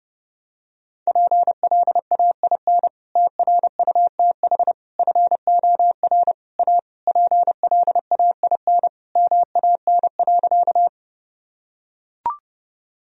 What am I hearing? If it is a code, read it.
Morse code, heard as a keyed tone, about 32 wpm, 700 Hz. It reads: PLAIN TRUTH FOR A PLAIN MAN.
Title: Sentences from Top 1000 Words - Review Part 2 - 30wpm